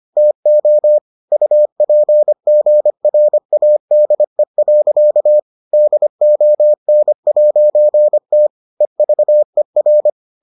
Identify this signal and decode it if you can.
Morse code, heard as a keyed tone, about 25 wpm, 600 Hz. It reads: TO UPGRADE. DON'T EVER